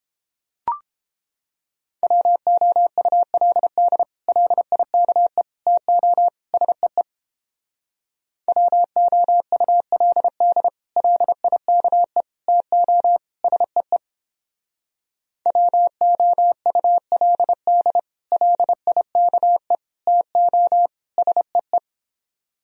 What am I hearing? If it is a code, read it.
Morse code, heard as a keyed tone, about 29 wpm, 700 Hz. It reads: WOULD LIKE TO SEE WOULD LIKE TO SEE WOULD LIKE TO SEE